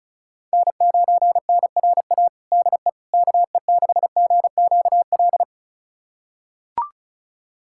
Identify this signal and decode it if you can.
Morse code, heard as a keyed tone, about 36 wpm, 700 Hz. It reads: N9NRA DE KE6GQL
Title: USA Callsign DE Callsign 35wpm